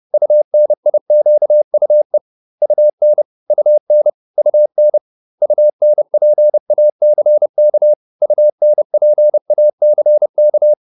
Morse code, heard as a keyed tone, about 30 wpm, 600 Hz. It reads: UNIQUE UN UN UN UNPACK UNPACK
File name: UN6 30